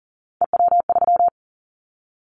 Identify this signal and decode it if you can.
Morse code, heard as a keyed tone, about 40 wpm, 690 Hz. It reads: EW3
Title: Callsign Prefixes for Top 100 DXCC - 40wpm